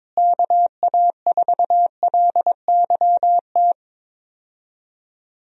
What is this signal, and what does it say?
Morse code, heard as a keyed tone, about 22 wpm, 700 Hz. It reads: KA4LYT
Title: USA Call Signs - 6 Characters - Repeated 3x - 22wpm